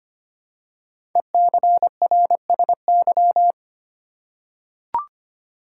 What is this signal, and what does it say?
Morse code, heard as a keyed tone, about 25 wpm, 700 Hz. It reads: ECRSY
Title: Code Groups - 5 Letters - 25wpm